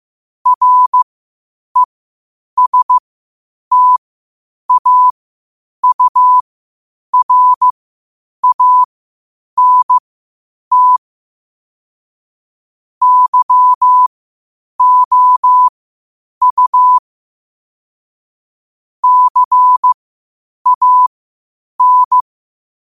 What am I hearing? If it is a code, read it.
Morse code, heard as a keyed tone, about 15 wpm, spaced as for 8 wpm, 1000 Hz. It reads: RESTAURANT YOU CAN
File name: qotdfiles / 8wpm